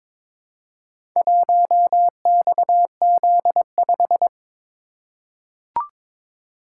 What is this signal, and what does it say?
Morse code, heard as a keyed tone, about 22 wpm, 700 Hz. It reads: 1XZ5
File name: practice / 22wpm / Four Letter-Number Combinations - 22wpm